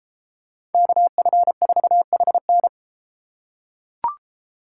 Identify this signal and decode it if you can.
Morse code, heard as a keyed tone, about 31 wpm, 700 Hz. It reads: KF4HN